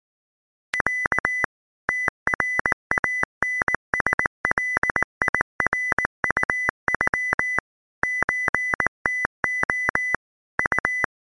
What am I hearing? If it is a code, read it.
Morse code (Russian alphabet), heard as a keyed tone, about 19 wpm, 2000 Hz. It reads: Я ТРАНСЛИРУЮ ЧТО У